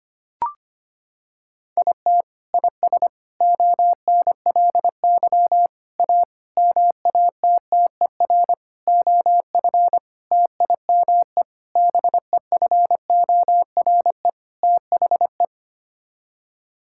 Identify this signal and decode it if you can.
Morse code, heard as a keyed tone, about 25 wpm, 700 Hz. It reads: IT IS ONLY A MATTER OF TIME BEFORE THE
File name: Sets of 9 Words - 25wpm